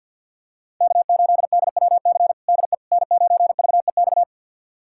Morse code, heard as a keyed tone, about 49 wpm, 700 Hz. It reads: K8DWC DE N9VEX
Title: USA Callsign DE Callsign - 5 Characters - 50wpm